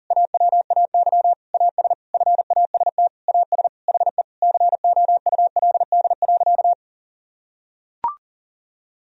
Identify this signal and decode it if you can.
Morse code, heard as a keyed tone, about 40 wpm, 700 Hz. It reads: AWAY AS FAST AS HE COULD.
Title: Sentences from Top 1000 Words - Review Part 2 - 40wpm